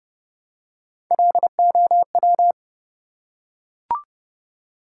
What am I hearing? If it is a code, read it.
Morse code, heard as a keyed tone, about 30 wpm, 700 Hz. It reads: LOW
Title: Three Letter Words - 30wpm